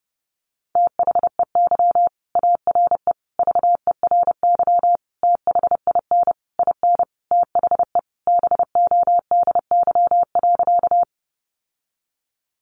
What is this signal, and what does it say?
Morse code, heard as a keyed tone, about 30 wpm, 700 Hz. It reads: THEY ARE VERY THIN IN THE BODY.
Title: Sentences from Top 1200 Words 30wpm